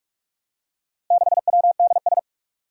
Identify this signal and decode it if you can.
Morse code, heard as a keyed tone, about 44 wpm, 700 Hz. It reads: 6WDS
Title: Four Challenging Letter-Number Combinations 45wpm